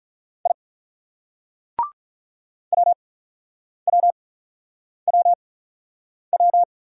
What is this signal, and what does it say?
Morse code, heard as a keyed tone, about 48 wpm, spaced as for 9 wpm, 700 Hz. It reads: I WWWW